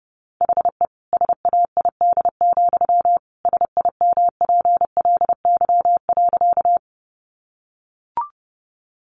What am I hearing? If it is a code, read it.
Morse code, heard as a keyed tone, about 30 wpm, 700 Hz. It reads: HE SAID, SIMPLY.
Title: Sentences from Top 600 Words 30wpm